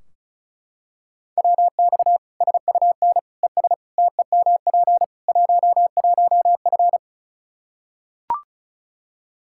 Morse code, heard as a keyed tone, about 36 wpm, 700 Hz. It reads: WX SUN ES TEMP 11F
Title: Practice QSO - Exchange 2 - Info - 35wpm